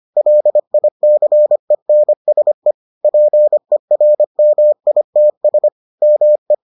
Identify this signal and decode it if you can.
Morse code, heard as a keyed tone, about 25 wpm, 600 Hz. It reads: LICENSE PERMITS ME